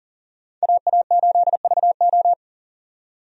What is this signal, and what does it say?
Morse code, heard as a keyed tone, about 40 wpm, 700 Hz. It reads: AA8VO
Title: USA Call Signs - 5 Characters - 40wpm